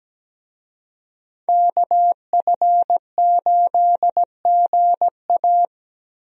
Morse code, heard as a keyed tone, about 17 wpm, 700 Hz. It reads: KF8GA
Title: USA Call Signs - 5 Characters - Repeated 2x - 17wpm